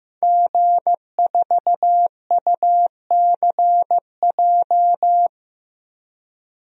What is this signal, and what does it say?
Morse code, heard as a keyed tone, about 15 wpm, 700 Hz. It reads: G4UCJ
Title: DXCC-100 Call Signs - 5 Characters - Repeated 2x - 15wpm